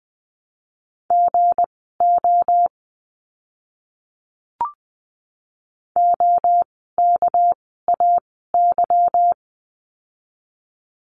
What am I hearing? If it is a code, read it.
Morse code, heard as a keyed tone, about 20 wpm, spaced as for 14 wpm, 700 Hz. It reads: GO OKAY
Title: 199 Lesson 32f - Letter k - Words 20wpm